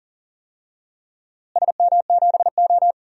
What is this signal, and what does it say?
Morse code, heard as a keyed tone, about 40 wpm, 700 Hz. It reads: SM7O
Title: DXCC-100 Call Signs - 4 Characters - Repeated 2x - 40wpm